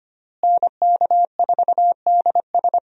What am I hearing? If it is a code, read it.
Morse code, heard as a keyed tone, about 25 wpm, 700 Hz. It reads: NK4DS
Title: USA Call Signs - 5 Characters - Repeated 2x - 25wpm